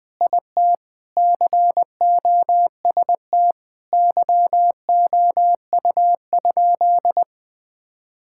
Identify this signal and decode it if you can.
Morse code, heard as a keyed tone, about 20 wpm, 700 Hz. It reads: IT COST YOU?
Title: Sentences from Top 1000 Words - Review Part 5 - 20wpm